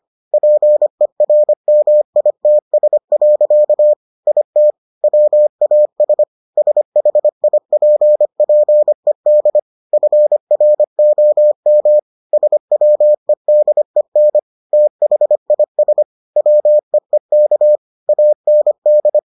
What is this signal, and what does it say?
Morse code, heard as a keyed tone, about 25 wpm, 600 Hz. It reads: PERMITS. IT WAS SHIPPED FROM SWEDEN THIS WEEK AND